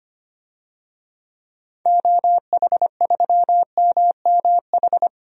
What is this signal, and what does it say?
Morse code, heard as a keyed tone, about 25 wpm, 700 Hz. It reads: OH3MMH
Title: DXCC-100 Call Signs - 6 Characters - 25wpm